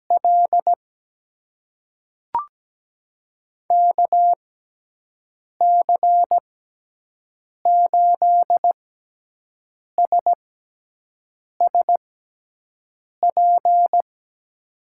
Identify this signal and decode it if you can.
Morse code, heard as a keyed tone, about 17 wpm, spaced as for 6 wpm, 700 Hz. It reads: L KC8SSP
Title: USA Call Signs - 6 Characters - 6x Character Spacing - 17wpm